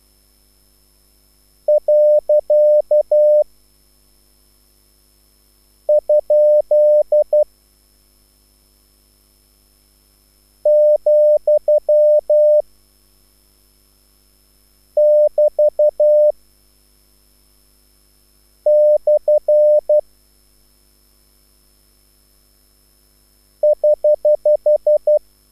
Morse code, heard as a keyed tone, about 12 wpm, 600 Hz. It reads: . ? , <BT> / <HH>
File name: QSL / CW lesson